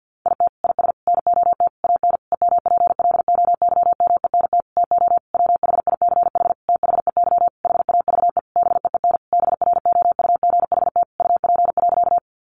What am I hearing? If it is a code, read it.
Morse code, heard as a keyed tone, about 50 wpm, 700 Hz. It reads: IT IS NOT AN EMPLOYMENT TO WHICH THEY HAVE BEEN BROUGHT UP.
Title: Sentences from Top 1500 Words 50wpm